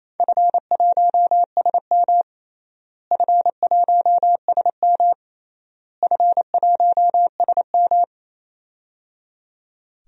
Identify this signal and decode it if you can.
Morse code, heard as a keyed tone, about 28 wpm, 700 Hz. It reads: F1SM F1SM F1SM
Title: DXCC-100 Call Signs - 4 Characters - Repeated 3x - 28wpm